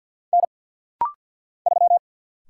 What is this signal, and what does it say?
Morse code, heard as a keyed tone, about 48 wpm, 700 Hz. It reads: N 3
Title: Single Letter-Number Alt - Rapid-Fire 50wpm